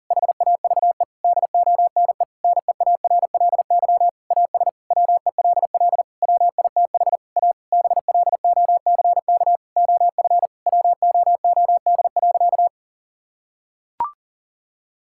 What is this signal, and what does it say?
Morse code, heard as a keyed tone, about 40 wpm, 700 Hz. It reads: HAVE DONE NEARLY AS WELL WITH A BLOCK OF WOOD.
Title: Sentences from Top 1600 Words 40wpm